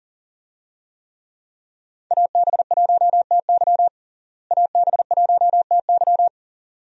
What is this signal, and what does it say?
Morse code, heard as a keyed tone, about 40 wpm, 700 Hz. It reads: AB1TY AB1TY
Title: USA Call Signs - 5 Characters - Repeated 2x - 40wpm